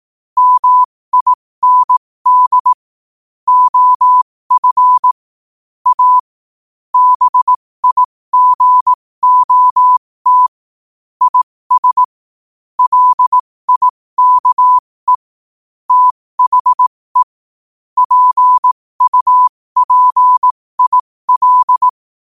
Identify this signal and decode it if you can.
Morse code, heard as a keyed tone, about 18 wpm, spaced as for 15 wpm, 1000 Hz. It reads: MIND OF A BIGOT IS LIKE THE PUPIL
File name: qotdfiles / 15wpm